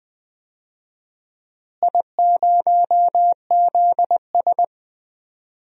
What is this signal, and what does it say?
Morse code, heard as a keyed tone, about 20 wpm, 700 Hz. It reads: I0ZS
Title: DXCC-100 Call Signs - 4 Characters - Repeated 3x - 20wpm